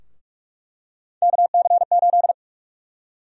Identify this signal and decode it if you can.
Morse code, heard as a keyed tone, about 46 wpm, 700 Hz. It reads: KC8
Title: USA Call Sign Prefixes - 45wpm